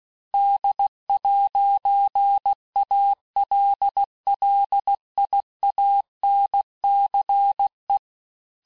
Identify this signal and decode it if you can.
Morse code, heard as a keyed tone, about 16 wpm, 800 Hz. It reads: D'ALLIANCE
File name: texte-clair-vitesse-15mpm-puis-20mpm